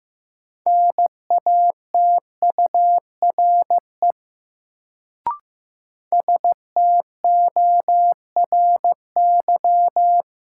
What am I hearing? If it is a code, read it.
Morse code, heard as a keyed tone, about 15 wpm, 700 Hz. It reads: NATURE STORY
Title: Sets of 3 Words - New Word Drill - 15wpm